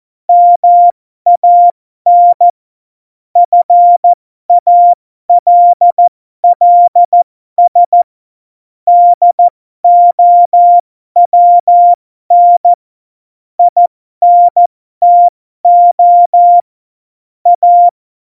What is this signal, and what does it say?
Morse code, heard as a keyed tone, about 14 wpm, spaced as for 12 wpm, 700 Hz. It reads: MAN FALLS DOWN INTO A
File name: Morsekurs / Witze